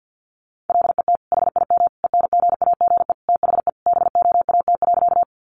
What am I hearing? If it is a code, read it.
Morse code, heard as a keyed tone, about 50 wpm, 700 Hz. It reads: LET HIM ENGAGE THE BOAT.
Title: Sentences from Top 1500 Words 50wpm